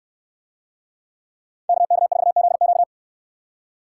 Morse code, heard as a keyed tone, about 68 wpm, 700 Hz. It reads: KK4ZQ